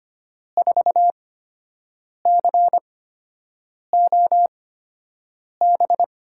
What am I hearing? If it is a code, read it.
Morse code, heard as a keyed tone, about 25 wpm, spaced as for 7 wpm, 700 Hz. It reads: 4COB
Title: USA Call Signs - 6 Characters - 8x Character Spacing - 25wpm